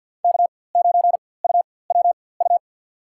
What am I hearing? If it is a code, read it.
Morse code, heard as a keyed tone, about 53 wpm, spaced as for 24 wpm, 700 Hz. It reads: K9UWU
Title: USA Call Signs - 5 Characters - 4x Character Spacing - 50wpm